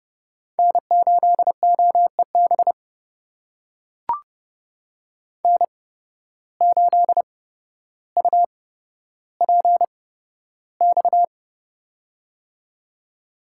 Morse code, heard as a keyed tone, about 30 wpm, spaced as for 13 wpm, 700 Hz. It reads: N8OEB N8UPX
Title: USA Call Signs - 5 Characters - 8x Character Spacing - 30wpm